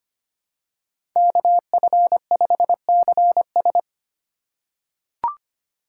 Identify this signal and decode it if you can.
Morse code, heard as a keyed tone, about 25 wpm, 700 Hz. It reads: KF5CS
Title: USA Call Signs - 5 Characters - 6x Character Spacing - 25wpm